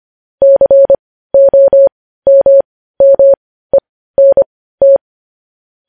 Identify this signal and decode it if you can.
Morse code, heard as a keyed tone, about 25 wpm, spaced as for 15 wpm, 550 Hz. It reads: COMMENT
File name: podcast / News headlines-1x-25x15